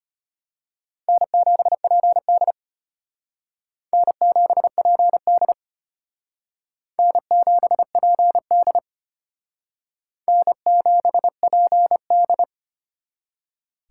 Morse code, heard as a keyed tone, about 31 wpm, 700 Hz. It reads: N7PD N7PD N7PD N7PD
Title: Speed Racing - USA 1x2 Call Signs 25wpm